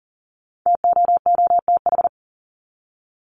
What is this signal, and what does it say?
Morse code, heard as a keyed tone, about 40 wpm, 700 Hz. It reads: TOOTH
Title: Top 1751-2000 Words 40wpm